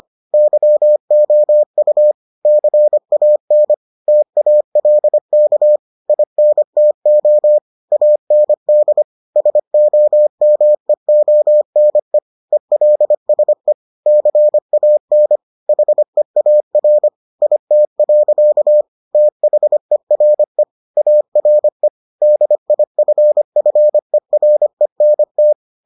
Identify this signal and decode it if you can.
Morse code, heard as a keyed tone, about 25 wpm, 600 Hz. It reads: YOU CAN TALK INTO AND SOMEONE ELSE CAN HEAR IT. THERE ARE DIFFERENT